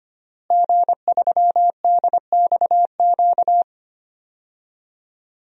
Morse code, heard as a keyed tone, about 25 wpm, 700 Hz. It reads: G3DXQ
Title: DXCC-100 Call Signs - 5 Characters - Repeated 3x - 25wpm